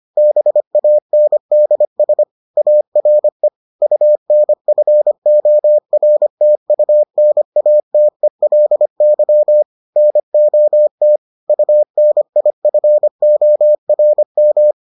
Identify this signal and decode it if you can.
Morse code, heard as a keyed tone, about 25 wpm, 600 Hz. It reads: BANDS ARE UNFORTUNATELY NOT UNIFORM